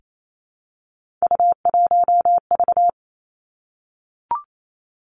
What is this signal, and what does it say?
Morse code, heard as a keyed tone, about 28 wpm, 700 Hz. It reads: U1V
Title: Three Challenging Letter-Number Combinations 28wpm